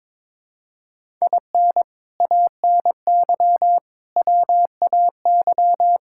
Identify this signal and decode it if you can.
Morse code, heard as a keyed tone, about 22 wpm, 700 Hz. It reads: IN ANY WAY
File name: practice / 22wpm / Sets of 6 Words - 22wpm